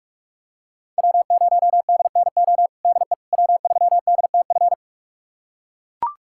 Morse code, heard as a keyed tone, about 42 wpm, 700 Hz. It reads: W0DNO DE W3DTF
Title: USA Callsign DE Callsign - 5 Characters - 45wpm